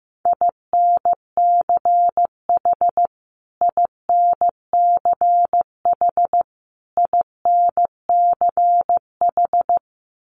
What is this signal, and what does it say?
Morse code, heard as a keyed tone, about 15 wpm, 700 Hz. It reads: INCH INCH INCH
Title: Top 500 Words Repeated 15wpm